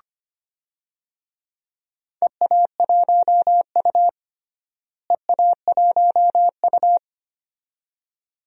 Morse code, heard as a keyed tone, about 25 wpm, 700 Hz. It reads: EA1U EA1U
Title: DXCC-100 Call Signs - 4 Characters - Repeated 2x - 25wpm